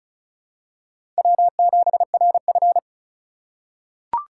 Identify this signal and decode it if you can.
Morse code, heard as a keyed tone, about 36 wpm, 690 Hz. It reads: W7RF
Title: USA Call Signs - 4 Characters - 3x Character Spacing - 35wpm